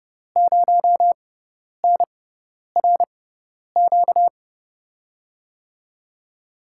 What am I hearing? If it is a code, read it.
Morse code, heard as a keyed tone, about 30 wpm, spaced as for 10 wpm, 700 Hz. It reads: 0NRQ
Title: USA Call Signs - 5 Characters - 6x Character Spacing - 30wpm